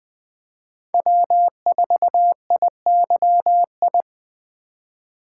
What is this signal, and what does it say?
Morse code, heard as a keyed tone, about 24 wpm, 700 Hz. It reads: W4IYI
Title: Speed Racing - USA Call Signs 20wpm